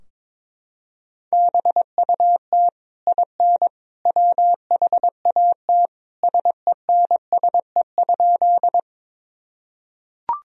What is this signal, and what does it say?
Morse code, heard as a keyed tone, about 22 wpm, 700 Hz. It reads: BUT IN WHAT SENSE?
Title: Sentences from Top 1000 Words - Review Part 4 - 22wpm